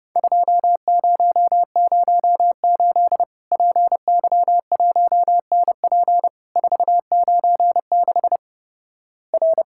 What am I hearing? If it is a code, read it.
Morse code, heard as a keyed tone, about 30 wpm, 700 Hz. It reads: 2008 PY1NP 496 R
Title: Worked All Europe DX Contest - QTC Exchange - 30wpm